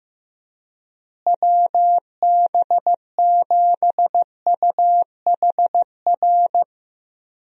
Morse code, heard as a keyed tone, about 15 wpm, 700 Hz. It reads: WB7UHR
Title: USA Call Signs - 6 Characters - Repeated 3x - 15wpm